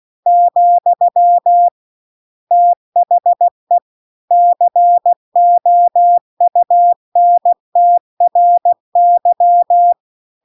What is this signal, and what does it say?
Morse code, heard as a keyed tone, about 16 wpm, 700 Hz. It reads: , THE COUNTRY